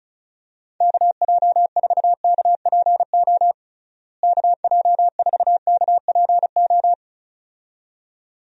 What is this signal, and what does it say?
Morse code, heard as a keyed tone, about 34 wpm, 700 Hz. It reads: KJ4KPO KJ4KPO
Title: USA Call Signs - 6 Characters - Repeated 2x - 35wpm